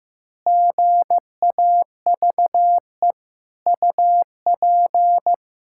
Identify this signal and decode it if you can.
Morse code, heard as a keyed tone, about 15 wpm, 700 Hz. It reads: GAVE UP
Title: Pangram Challenge v2 - 15wpm